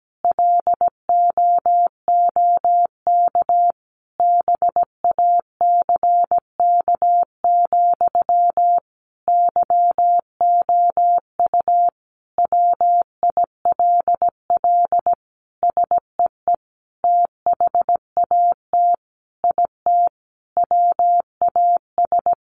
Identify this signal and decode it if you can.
Morse code, heard as a keyed tone, about 17 wpm, 700 Hz. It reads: LOOK BACK, YOU WILL SEE THAT IT WAS
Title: Sentences from Top 800 Words 17wpm